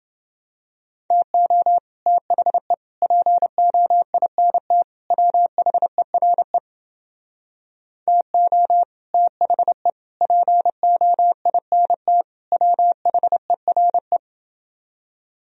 Morse code, heard as a keyed tone, about 29 wpm, 700 Hz. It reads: TO THE POINT WHERE TO THE POINT WHERE